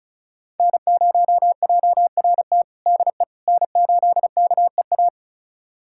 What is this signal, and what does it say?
Morse code, heard as a keyed tone, about 34 wpm, 700 Hz. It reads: N0JRT DE N8KEA
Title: USA Callsign DE Callsign - 5 Characters - 35wpm